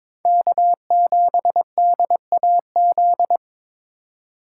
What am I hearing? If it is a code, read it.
Morse code, heard as a keyed tone, about 21 wpm, 700 Hz. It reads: K7DAZ
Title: USA Call Signs - 5 Characters - 2x Character Spacing - 22wpm